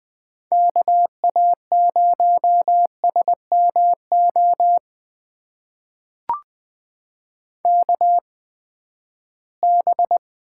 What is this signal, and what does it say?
Morse code, heard as a keyed tone, about 20 wpm, spaced as for 14 wpm, 700 Hz. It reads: KA0SMO KB
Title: USA Call Signs - 6 Characters - 8x Character Spacing - 20wpm